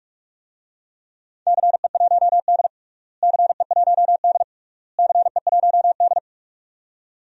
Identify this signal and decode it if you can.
Morse code, heard as a keyed tone, about 46 wpm, 700 Hz. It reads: CE1D CE1D CE1D
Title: DXCC-100 Call Signs - 4 Characters - Repeated 3x - 45wpm